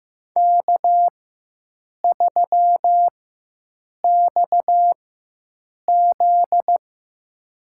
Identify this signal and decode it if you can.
Morse code, heard as a keyed tone, about 15 wpm, spaced as for 7 wpm, 700 Hz. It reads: K3XZ
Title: USA Call Signs - 4 Characters - 4x Character Spacing - 15wpm